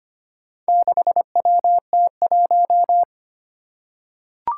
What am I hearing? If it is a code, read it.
Morse code, heard as a keyed tone, about 25 wpm, 700 Hz. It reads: 6WT1
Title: Four Challenging Letter-Number Combinations 25wpm